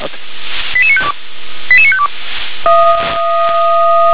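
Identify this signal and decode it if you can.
Five-tone selective call (ZVEI), heard as quick runs of five tones, tones of about 70 ms. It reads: 88952 88952
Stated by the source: ZVEI five-tone call received by SDR.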